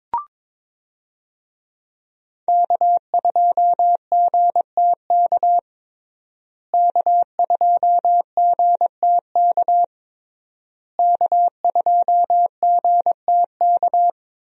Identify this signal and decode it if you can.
Morse code, heard as a keyed tone, about 22 wpm, 700 Hz. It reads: K2GTK K2GTK K2GTK
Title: USA Call Signs - 5 Characters - Repeated 3x - 22wpm